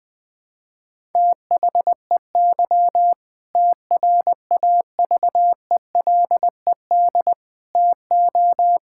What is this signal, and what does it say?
Morse code, heard as a keyed tone, about 20 wpm, 700 Hz. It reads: THEY TRAVELED TO
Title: Sentences with 2-Word Phrases - 20wpm